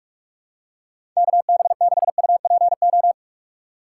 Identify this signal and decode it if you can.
Morse code, heard as a keyed tone, about 46 wpm, 700 Hz. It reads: KB6UPO
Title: USA Call Signs - 6 Characters - 8x Character Spacing - 45wpm